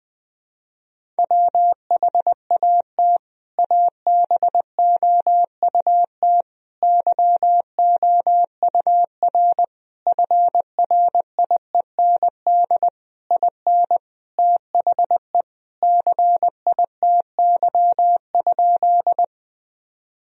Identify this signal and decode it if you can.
Morse code, heard as a keyed tone, about 21 wpm, 700 Hz. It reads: WHAT ABOUT YOUR FRIEND IN THE CITY?
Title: Sentences from Top 1000 Words - Review Part 1 - 20wpm